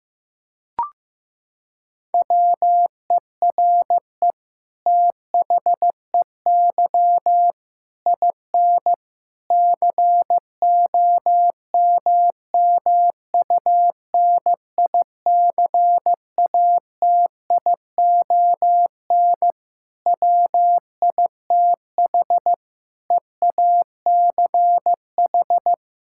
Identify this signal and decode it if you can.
Morse code, heard as a keyed tone, about 15 wpm, 700 Hz. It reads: WERE THEY IN COMMUNICATION WITH EACH
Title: Sentences from Top 1000 Words - Review Part 6 - 15wpm